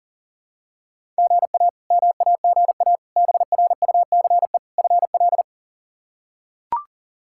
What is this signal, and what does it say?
Morse code, heard as a keyed tone, about 40 wpm, 700 Hz. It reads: GA MAGA BRUCE FL
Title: K1USN SST QSO Practice - 40wpm